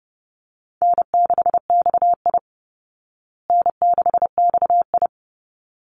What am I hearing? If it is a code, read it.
Morse code, heard as a keyed tone, about 30 wpm, 700 Hz. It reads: N6XI N6XI
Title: Easy Call Signs Repeated 2x - 30wpm